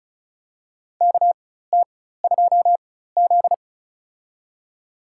Morse code, heard as a keyed tone, about 34 wpm, spaced as for 16 wpm, 700 Hz. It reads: KT2Z
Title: USA Call Signs - 4 Characters - 4x Character Spacing - 35wpm